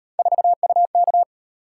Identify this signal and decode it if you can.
Morse code, heard as a keyed tone, about 35 wpm, 700 Hz. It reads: 4UK